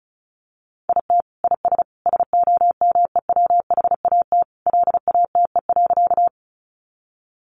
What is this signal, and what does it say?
Morse code, heard as a keyed tone, about 34 wpm, 700 Hz. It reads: IT IS SOMEWHAT LATE.
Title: Sentences from Top 1300 Words 35wpm